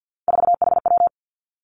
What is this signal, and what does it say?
Morse code, heard as a keyed tone, about 50 wpm, 700 Hz. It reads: 4HW
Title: Three Challenging Letter-Number Combinations 50wpm